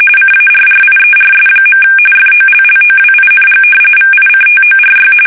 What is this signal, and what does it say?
RTTY, about 45.45 baud. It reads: CARRIERS. FOLLOWING VICTORY
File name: fsk850